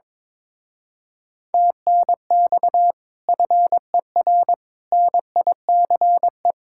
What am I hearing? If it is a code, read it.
Morse code, heard as a keyed tone, about 22 wpm, 700 Hz. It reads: TNX FER NICE
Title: Speed-Racing Common QSO Phrases Encore - 22wpm